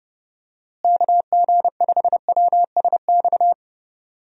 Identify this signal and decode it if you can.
Morse code, heard as a keyed tone, about 30 wpm, 700 Hz. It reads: KG5WSX
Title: USA Call Signs - 6 Characters - Repeated 3x - 30wpm